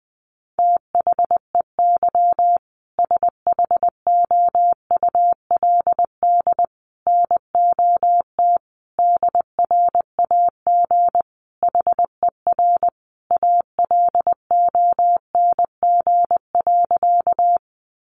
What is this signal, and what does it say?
Morse code, heard as a keyed tone, about 20 wpm, 700 Hz. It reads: THEY SHOULD NOT DRAG HER ALONG.
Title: Sentences from Top 1500 Words 20wpm